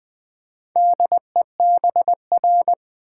Morse code, heard as a keyed tone, about 20 wpm, 700 Hz. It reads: DEBR
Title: Build A Word - 2 Letters At A Time - 6-Letter Words - 20wpm